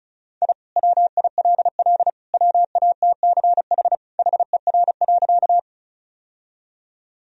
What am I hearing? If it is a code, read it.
Morse code, heard as a keyed tone, about 36 wpm, 700 Hz. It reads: I WILL WATCH HER.
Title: Sentences from Top 1000 Words - Review Part 1 - 35wpm